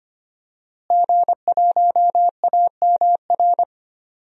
Morse code, heard as a keyed tone, about 25 wpm, 700 Hz. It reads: G1AMR
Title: DXCC-100 Call Signs - 5 Characters - Repeated 3x - 25wpm